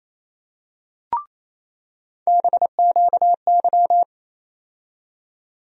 Morse code, heard as a keyed tone, about 28 wpm, 700 Hz. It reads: BQY
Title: USA Call Sign Postfixes - 3 Letters 28wpm